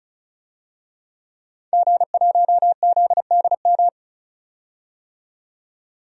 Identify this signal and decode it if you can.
Morse code, heard as a keyed tone, about 36 wpm, 700 Hz. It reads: G1ZDM
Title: DXCC-100 Call Signs - 5 Characters - 35wpm